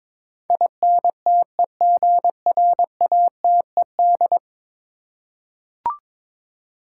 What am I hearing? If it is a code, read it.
Morse code, heard as a keyed tone, about 23 wpm, 700 Hz. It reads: INTEGRATED
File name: practice / 22wpm / 10 Letter Words - First 3 Letters Given - 22wpm